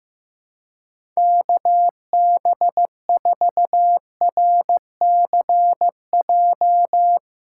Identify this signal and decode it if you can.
Morse code, heard as a keyed tone, about 15 wpm, 700 Hz. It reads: KB4RCJ
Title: USA Call Signs - 4x Character Spacing - 15wpm